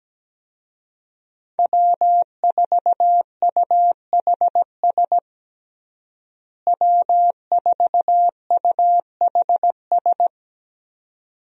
Morse code, heard as a keyed tone, about 17 wpm, 700 Hz. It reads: W4UHS W4UHS
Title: USA Call Signs - 5 Characters - Repeated 2x - 17wpm